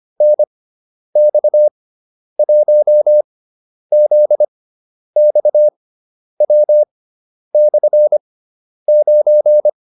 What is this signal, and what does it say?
Morse code, heard as a keyed tone, about 25 wpm, spaced as for 10 wpm, 600 Hz. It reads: NX1ZXW/9